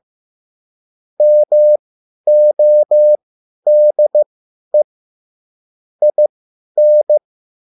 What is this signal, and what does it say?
Morse code, heard as a keyed tone, about 15 wpm, spaced as for 10 wpm, 600 Hz. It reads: MODE IN
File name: cwbook